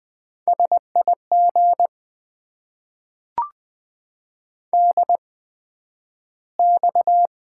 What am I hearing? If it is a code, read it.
Morse code, heard as a keyed tone, about 20 wpm, spaced as for 11 wpm, 700 Hz. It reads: SIG DX
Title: Top 90 QSO Elements - 8x Character Spacing - 20wpm